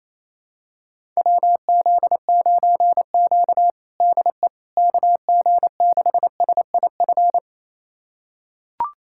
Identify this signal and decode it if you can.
Morse code, heard as a keyed tone, about 28 wpm, 700 Hz. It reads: WZ9Q DE KG6SIF
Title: USA Callsign DE Callsign 28wpm